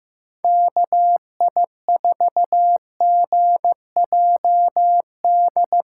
Morse code, heard as a keyed tone, about 15 wpm, 700 Hz. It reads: KI4GJD
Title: USA Call Signs - 6 Characters - 3x Character Spacing - 15wpm